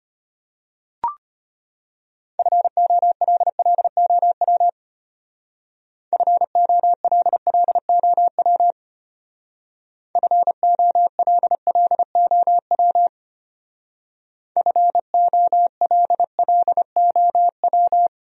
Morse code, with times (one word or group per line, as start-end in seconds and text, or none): 2.4-4.7 FOLLOW
6.1-8.7 FOLLOW
10.2-13.1 FOLLOW
14.6-18.1 FOLLOW